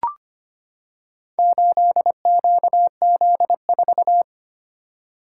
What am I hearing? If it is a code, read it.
Morse code, heard as a keyed tone, about 25 wpm, 700 Hz. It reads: 8QZ4
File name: practice / 25wpm / Four Letter-Number Combinations - 25wpm